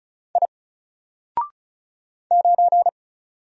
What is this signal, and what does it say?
Morse code, heard as a keyed tone, about 33 wpm, 700 Hz. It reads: I 9
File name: Single-Letter-Number-35wpm